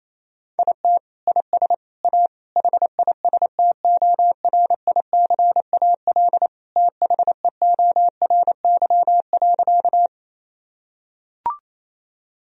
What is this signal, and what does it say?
Morse code, heard as a keyed tone, about 28 wpm, 700 Hz. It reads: IT IS A HISTORICAL THEORY.
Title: Sentences from Top 1000 Words - Review Part 4 - 28wpm